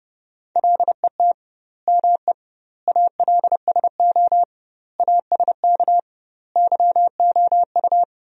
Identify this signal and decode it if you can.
Morse code, heard as a keyed tone, about 30 wpm, 700 Hz. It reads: LET ME ALSO ASK YOU
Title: 2x Word-Spacing - Sentences from Top 400 Words - 30wpm